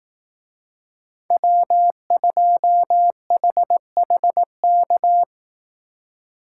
Morse code, heard as a keyed tone, about 18 wpm, 700 Hz. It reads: W2HHK
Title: Speed Racing - USA 1x3 Call Signs 15wpm